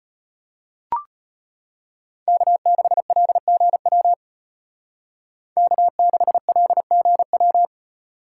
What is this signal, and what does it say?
Morse code, heard as a keyed tone, about 33 wpm, 700 Hz. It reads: K6LGW K6LGW